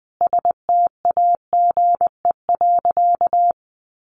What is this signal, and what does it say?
Morse code, heard as a keyed tone, about 20 wpm, 700 Hz. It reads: STAGE.
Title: Sentences from Top 1200 Words 20wpm